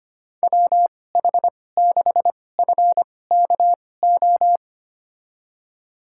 Morse code, heard as a keyed tone, about 25 wpm, spaced as for 18 wpm, 700 Hz. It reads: WH6FKO
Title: USA Call Signs - 6 Characters - 2x Character Spacing - 25wpm